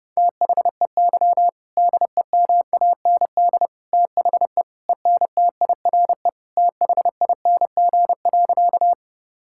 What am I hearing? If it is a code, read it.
Morse code, heard as a keyed tone, about 30 wpm, 700 Hz. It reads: THEY DEMAND THE ENTIRE THING.
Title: Sentences from Top 1000 Words - Review Part 5 - 30wpm